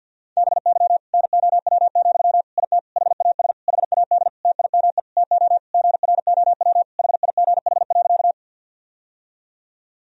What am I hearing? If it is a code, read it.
Morse code, heard as a keyed tone, about 48 wpm, 700 Hz. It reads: BY NOW, IT HAS HAD TIME TO GROW HIGH.
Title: Sentences from Top 1000 Words - Review Part 1 - 50wpm